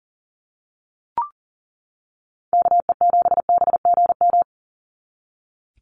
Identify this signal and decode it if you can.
Morse code, heard as a keyed tone, about 40 wpm, 700 Hz. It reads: KE7BGM
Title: NA SOTA Activator Callsigns 40wpm